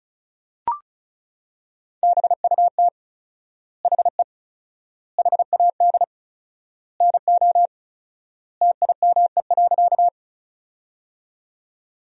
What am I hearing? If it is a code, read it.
Morse code, heard as a keyed tone, about 36 wpm, 700 Hz. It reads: BUT HE HAD NO TIME.
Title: 4x Word-Spacing - Sentences from Top 100 Words - 35wpm